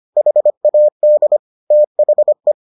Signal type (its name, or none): Morse code